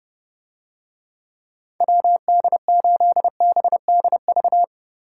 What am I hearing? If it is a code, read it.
Morse code, heard as a keyed tone, about 32 wpm, 700 Hz. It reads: WD8BDV